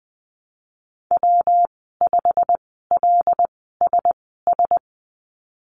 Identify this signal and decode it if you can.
Morse code, heard as a keyed tone, about 20 wpm, spaced as for 15 wpm, 700 Hz. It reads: W5LSS
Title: 179 Lesson 29g - Slash - Call Signs 20wpm